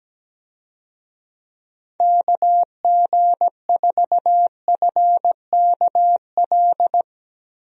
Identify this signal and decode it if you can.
Morse code, heard as a keyed tone, about 17 wpm, 700 Hz. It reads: KG4FKL
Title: USA Call Signs - 6 Characters - Repeated 2x - 17wpm